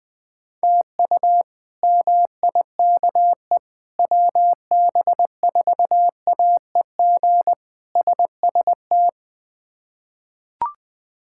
Morse code, heard as a keyed tone, about 20 wpm, 700 Hz. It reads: TU MIKE WB4AEG SST
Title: K1USN SST QSO Practice - 20wpm